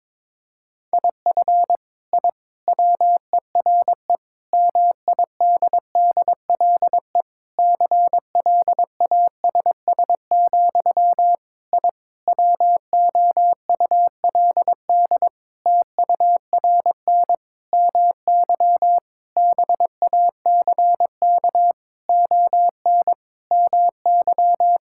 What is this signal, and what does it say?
Morse code, heard as a keyed tone, about 22 wpm, 700 Hz. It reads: IF I WERE MIDDLE CLASS, I WOULD TURN MY BACK ON MY
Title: Sentences from Top 1000 Words - Review Part 6 - 22wpm